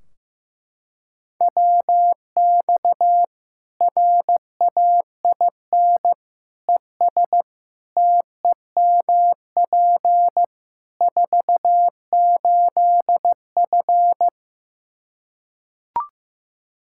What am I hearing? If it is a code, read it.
Morse code, heard as a keyed tone, about 15 wpm, 700 Hz. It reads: WX RAIN ES TEMP 48F
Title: Practice QSO - Exchange 2 - Info - 15wpm